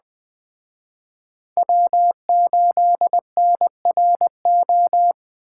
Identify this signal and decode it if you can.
Morse code, heard as a keyed tone, about 21 wpm, 700 Hz. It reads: W8NRO
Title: Speed Racing - USA 1x3 Call Signs 17wpm